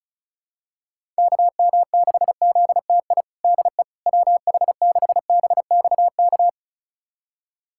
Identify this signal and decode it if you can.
Morse code, heard as a keyed tone, about 36 wpm, 700 Hz. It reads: KM6ZTI DE WH6BXK
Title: USA Callsign DE Callsign - 6 Characters - 35wpm